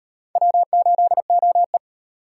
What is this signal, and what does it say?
Morse code, heard as a keyed tone, about 37 wpm, 700 Hz. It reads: W8OE